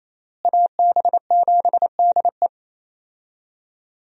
Morse code, heard as a keyed tone, about 27 wpm, 700 Hz. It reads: AB7DE
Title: USA Call Signs - 5 Characters - Repeated 3x - 28wpm